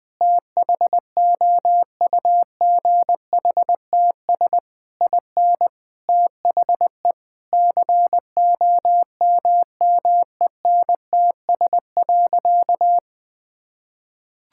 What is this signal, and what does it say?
Morse code, heard as a keyed tone, about 20 wpm, 700 Hz. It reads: THOUGHTS IN THE COMMENTS.
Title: Sentences with 8-Word Phrases - Spoken Beginning - 20wpm